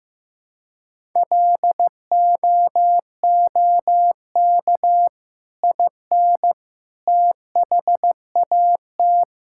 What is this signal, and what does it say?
Morse code, heard as a keyed tone, about 16 wpm, 700 Hz. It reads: LOOK IN THAT
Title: Sentences from Top 1000 Words - Review Part 5 - 15wpm